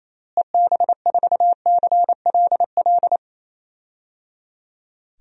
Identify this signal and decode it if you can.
Morse code, heard as a keyed tone, about 28 wpm, 700 Hz. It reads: EB4CLL
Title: DXCC-100 Call Signs - 6 Characters - Repeated 3x - 28wpm